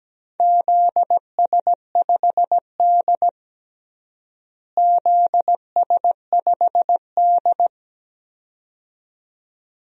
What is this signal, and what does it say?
Morse code, heard as a keyed tone, about 17 wpm, 700 Hz. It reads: ZS5D ZS5D
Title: DXCC-100 Call Signs - 4 Characters - Repeated 2x - 17wpm